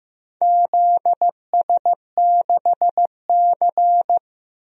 Morse code, heard as a keyed tone, about 15 wpm, 700 Hz. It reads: ZS6C
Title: DXCC-100 Call Signs - 4 Characters - Repeated 3x - 15wpm